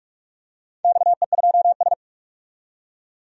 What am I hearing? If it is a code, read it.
Morse code, heard as a keyed tone, about 45 wpm, 700 Hz. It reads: XE2S
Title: DXCC-100 Call Signs - 4 Characters - 45wpm